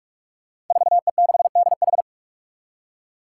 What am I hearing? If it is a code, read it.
Morse code, heard as a keyed tone, about 46 wpm, 700 Hz. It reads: 4E6DH
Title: Five Challenging Letter-Number Combinations 45wpm